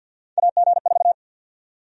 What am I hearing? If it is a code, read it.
Morse code, heard as a keyed tone, about 52 wpm, 700 Hz. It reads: AG4
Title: USA Call Sign Prefixes - 50wpm